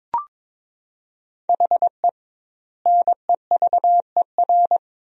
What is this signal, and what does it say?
Morse code, heard as a keyed tone, about 23 wpm, 700 Hz. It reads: HE NEVER
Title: 2x Word-Spacing - Sentences from Top 400 Words - 22wpm